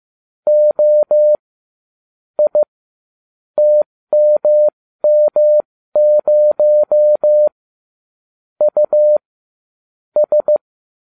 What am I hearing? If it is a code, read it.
Morse code, heard as a keyed tone, about 15 wpm, spaced as for 13 wpm, 600 Hz. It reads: O I TMM0 U S